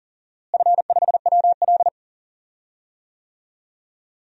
Morse code, heard as a keyed tone, about 40 wpm, 700 Hz. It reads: F5WL
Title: DXCC-100 Call Signs - 4 Characters - Repeated 3x - 40wpm